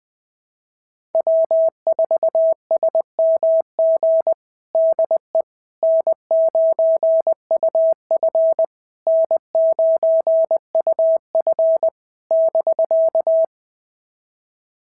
Morse code, heard as a keyed tone, about 21 wpm, 660 Hz. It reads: W4SMG DE N9UF N9UF <BK>
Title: SOTA QSO Practice 20wpm